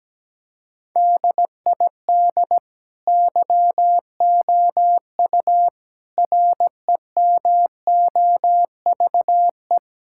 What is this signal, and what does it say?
Morse code, heard as a keyed tone, about 17 wpm, 700 Hz. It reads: DID YOU REMOVE
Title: Sentences from Top 1000 Words - Review Part 5 - 17wpm